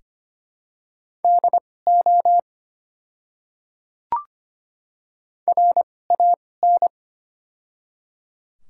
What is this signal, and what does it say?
Morse code, heard as a keyed tone, about 25 wpm, spaced as for 18 wpm, 700 Hz. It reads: DO RAN
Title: 2x Character Spacing - Top 500 Words - 25wpm